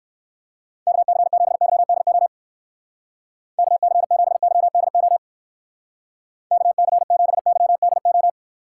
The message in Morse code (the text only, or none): KC7YDO KC7YDO KC7YDO